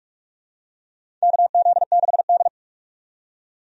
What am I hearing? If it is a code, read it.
Morse code, heard as a keyed tone, about 44 wpm, 700 Hz. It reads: KZ6D
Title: USA Call Signs - 4 Characters - 45wpm